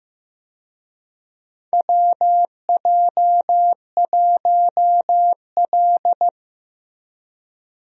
Morse code, heard as a keyed tone, about 15 wpm, 700 Hz. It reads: WJ1L
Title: USA Call Signs - 4 Characters - Repeated 2x - 15wpm